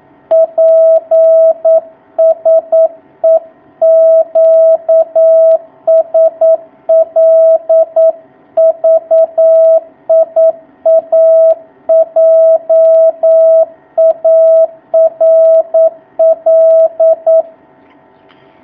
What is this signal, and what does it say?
Morse code, heard as a keyed tone, about 9 wpm, 650 Hz. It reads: PSE QSL VIA JARL
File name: jh8cbh / 2019cw